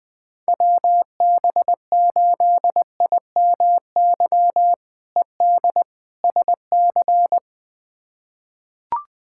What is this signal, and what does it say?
Morse code, heard as a keyed tone, about 20 wpm, 700 Hz. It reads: WB8IMY ED SC
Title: K1USN SST QSO Practice - 20wpm